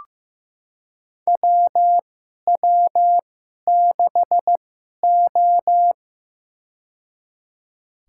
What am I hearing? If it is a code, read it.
Morse code, heard as a keyed tone, about 16 wpm, spaced as for 11 wpm, 700 Hz. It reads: WW6O
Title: USA Call Signs - 4 Characters - 2x Character Spacing - 15wpm